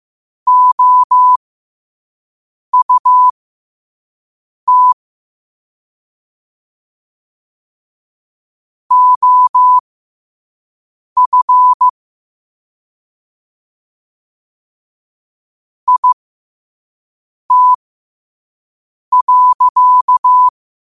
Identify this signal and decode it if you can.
Morse code, heard as a keyed tone, about 15 wpm, spaced as for 5 wpm, 1000 Hz. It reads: OUT OF IT.
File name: qotdfiles / 5wpm